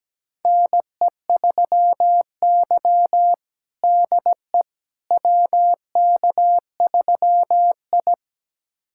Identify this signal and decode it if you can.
Morse code, heard as a keyed tone, about 17 wpm, 700 Hz. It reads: NE3Y DE WK3I
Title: USA Callsign DE Callsign - 4 Characters - 17wpm